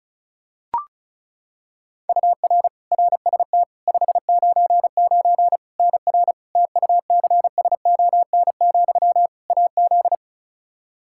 Morse code, heard as a keyed tone, about 36 wpm, 700 Hz. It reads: UR RST 599 NR TUCSON, AZ
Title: Practice QSO - Exchange 1 - Beginning - 35wpm